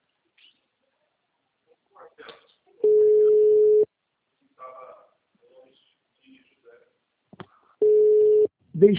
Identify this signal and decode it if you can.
Ringback tone (Europe, 425 Hz), with speech.